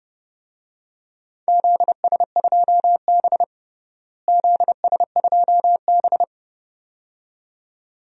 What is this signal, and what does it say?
Morse code, heard as a keyed tone, about 30 wpm, 700 Hz. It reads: ZS2B ZS2B
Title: DXCC-100 Call Signs - 4 Characters - Repeated 2x - 30wpm